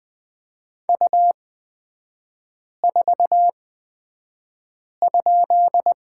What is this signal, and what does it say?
Morse code, heard as a keyed tone, about 20 wpm, spaced as for 5 wpm, 700 Hz. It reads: U4?
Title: 224 Lesson 36e - BT - Three Characters 20wpm